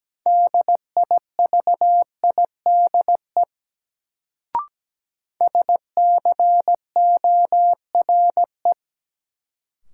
Morse code, heard as a keyed tone, about 17 wpm, 700 Hz. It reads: DIVIDE SCORE
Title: Sentences from Top 1600 Words - New Word Drill Alt - 17wpm